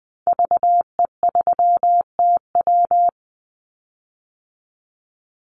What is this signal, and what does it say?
Morse code, heard as a keyed tone, about 20 wpm, 700 Hz. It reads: VE3TW
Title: Call Signs Repeated 3x - 20wpm